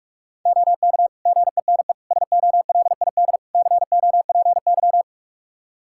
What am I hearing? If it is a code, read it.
Morse code, heard as a keyed tone, about 41 wpm, 700 Hz. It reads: OK GENE SOLID COPY
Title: Practice QSO - Exchange 3 - More Info - 45wpm